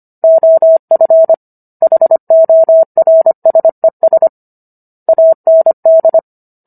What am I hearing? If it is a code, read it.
Morse code, heard as a keyed tone, about 25 wpm, 650 Hz. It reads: OF HORSES AND